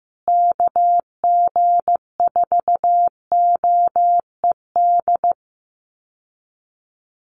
Morse code, heard as a keyed tone, about 15 wpm, 700 Hz. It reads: KG4OED
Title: USA Call Signs Repeated 3x 15wpm